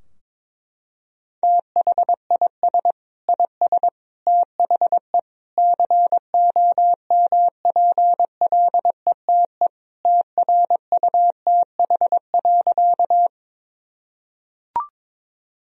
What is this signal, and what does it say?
Morse code, heard as a keyed tone, about 22 wpm, 700 Hz. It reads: THIS IS THE COMPLETE TRUTH.
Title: Sentences from Top 1000 Words - Review Part 2 - 22wpm